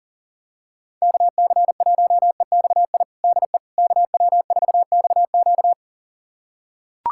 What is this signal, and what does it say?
Morse code, heard as a keyed tone, about 37 wpm, 700 Hz. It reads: KC1EXI DE KW4XQ
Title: USA Callsign DE Callsign 40wpm